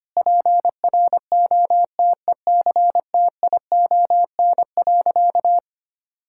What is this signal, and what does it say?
Morse code, heard as a keyed tone, about 25 wpm, 700 Hz. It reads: PROTECTION.
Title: Sentences from Top 1600 Words 25wpm